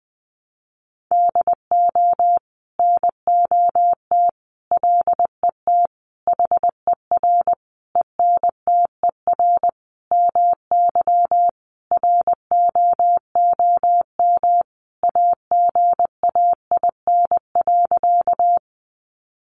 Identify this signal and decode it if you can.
Morse code, heard as a keyed tone, about 20 wpm, 700 Hz. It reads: DO NOT LET HER ENTER MY ROOM AGAIN.
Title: Sentences from Top 1300 Words 20wpm